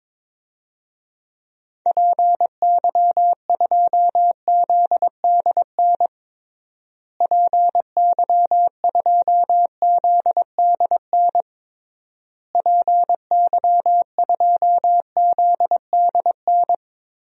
Morse code, heard as a keyed tone, about 22 wpm, 700 Hz. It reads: PY2ZDN PY2ZDN PY2ZDN
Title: DXCC-100 Call Signs - 6 Characters - Repeated 3x - 22wpm